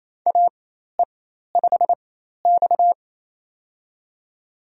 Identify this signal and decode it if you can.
Morse code, heard as a keyed tone, about 28 wpm, spaced as for 13 wpm, 700 Hz. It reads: AE5X
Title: USA Call Signs - 4x Character Spacing - 28wpm